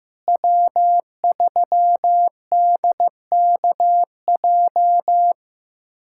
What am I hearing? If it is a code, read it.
Morse code, heard as a keyed tone, about 15 wpm, 700 Hz. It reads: W3DKJ
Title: USA Callsign DE Callsign - 5 Characters - 15wpm